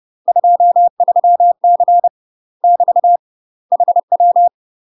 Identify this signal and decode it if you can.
Morse code, heard as a keyed tone, about 30 wpm, 700 Hz. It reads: 23C <BT> HW
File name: UK MASTER QSOs TRAINING / 30WPM / UK MASTER QSOs 0013